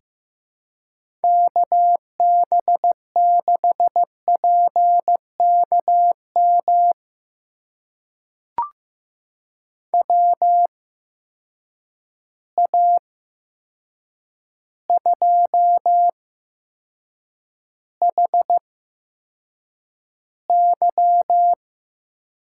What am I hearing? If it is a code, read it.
Morse code, heard as a keyed tone, about 15 wpm, spaced as for 7 wpm, 700 Hz. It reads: KB6PKM WA2HY
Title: USA Call Signs - 6 Characters - 8x Character Spacing - 15wpm